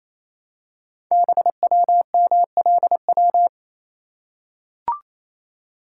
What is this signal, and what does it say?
Morse code, heard as a keyed tone, about 28 wpm, 700 Hz. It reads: BWMLW
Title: Code Groups - 5 Letters - 28wpm